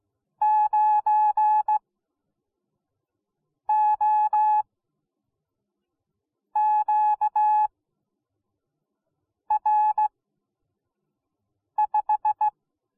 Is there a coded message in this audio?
Yes: Morse code from 0.4 to 12.5 s, heard as a keyed tone, about 17 wpm, spaced as for 4 wpm, 850 Hz. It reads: 9OQR5